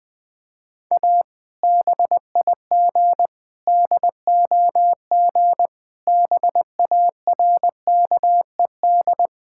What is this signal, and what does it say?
Morse code, heard as a keyed tone, about 20 wpm, 700 Hz. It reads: A BIG DOG BARKED
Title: Sentences with 2-Word Phrases - Spoken Beginning - 20wpm